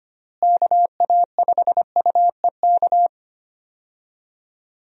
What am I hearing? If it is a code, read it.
Morse code, heard as a keyed tone, about 25 wpm, 700 Hz. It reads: KA5UEK
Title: USA Call Signs - 6 Characters - Repeated 3x - 25wpm